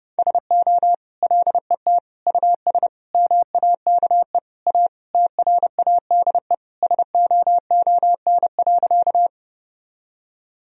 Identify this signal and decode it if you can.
Morse code, heard as a keyed tone, about 32 wpm, 700 Hz. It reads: SO LET US MAKE A TRADE SOON.
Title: Sentences from Top 1000 Words - Review Part 4 - 30wpm